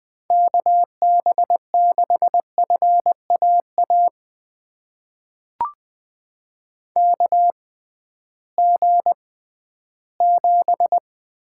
Morse code, heard as a keyed tone, about 20 wpm, spaced as for 13 wpm, 700 Hz. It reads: KB6FAA KG7
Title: USA Call Signs - 6 Characters - 6x Character Spacing - 20wpm